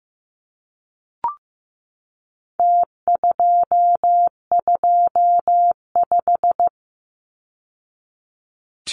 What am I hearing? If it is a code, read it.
Morse code, heard as a keyed tone, about 15 wpm, 700 Hz. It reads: T225
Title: Numbers - Contest Style - 4-Digit 15wpm